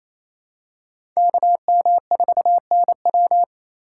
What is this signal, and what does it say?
Morse code, heard as a keyed tone, about 28 wpm, 700 Hz. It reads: KM4NW
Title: USA Call Signs - 4x Character Spacing - 28wpm